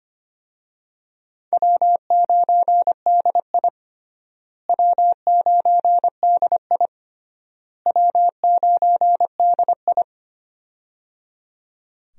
Morse code, heard as a keyed tone, about 25 wpm, 700 Hz. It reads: W9DI W9DI W9DI
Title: USA Call Signs - 4 Characters - Repeated 3x - 25wpm